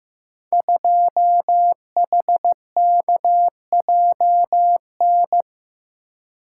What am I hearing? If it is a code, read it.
Morse code, heard as a keyed tone, about 15 wpm, 700 Hz. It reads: 2HKJN
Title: Code Groups - 5 Letter:Numbers - 15wpm